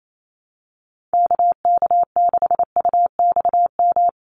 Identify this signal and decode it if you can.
Morse code, heard as a keyed tone, about 28 wpm, 700 Hz. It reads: KK6UXM
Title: USA Call Signs Repeated 2x 28wpm